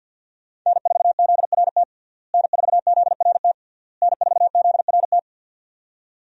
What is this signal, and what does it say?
Morse code, heard as a keyed tone, about 52 wpm, 700 Hz. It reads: N4ZRT N4ZRT N4ZRT
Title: USA Call Signs - 5 Characters - Repeated 3x - 50wpm